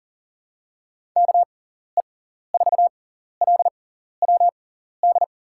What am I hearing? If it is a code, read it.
Morse code, heard as a keyed tone, about 40 wpm, spaced as for 14 wpm, 700 Hz. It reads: KE4LWD
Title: USA Call Signs - 6 Characters - 6x Character Spacing - 40wpm